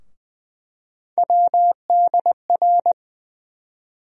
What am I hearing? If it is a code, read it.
Morse code, heard as a keyed tone, about 20 wpm, 700 Hz. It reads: WDR
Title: USA Call Sign Postfixes - 3 Letters 20wpm